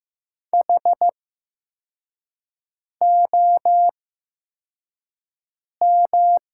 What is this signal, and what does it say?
Morse code, heard as a keyed tone, about 15 wpm, spaced as for 4 wpm, 700 Hz. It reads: HOM
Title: 8x Character Spacing - Top 200 Words - 15wpm